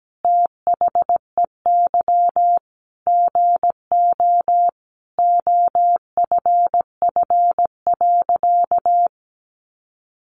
Morse code, heard as a keyed tone, about 17 wpm, 700 Hz. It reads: THEY GO OFF.
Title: Sentences from Top 600 Words 17wpm